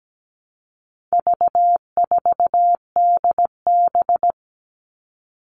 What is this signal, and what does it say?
Morse code, heard as a keyed tone, about 17 wpm, 700 Hz. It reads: V4DB
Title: All Cut Numbers - 4-Digit 17wpm